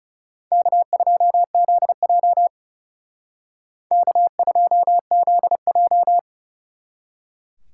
Morse code, heard as a keyed tone, about 38 wpm, 700 Hz. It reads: K2ZJ K2ZJ